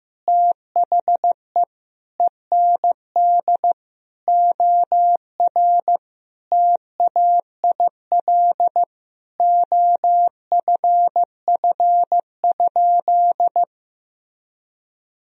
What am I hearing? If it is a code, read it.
Morse code, heard as a keyed tone, about 15 wpm, 700 Hz. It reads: THE END OR TAIL OFF?
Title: Sentences from Top 1000 Words - Review Part 3 - 15wpm